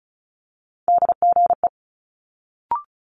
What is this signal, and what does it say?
Morse code, heard as a keyed tone, about 36 wpm, 700 Hz. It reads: DGE
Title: Multi-letter Phonemes Quick Fire 35wpm